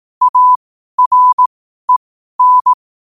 Morse code, heard as a keyed tone, about 18 wpm, spaced as for 12 wpm, 1000 Hz. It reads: AREN
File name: qotdfiles / 12wpm